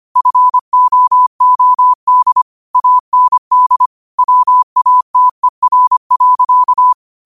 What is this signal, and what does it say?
Morse code, heard as a keyed tone, about 25 wpm, 1000 Hz. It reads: FOOD AND WATER.
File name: qotdfiles / 25wpm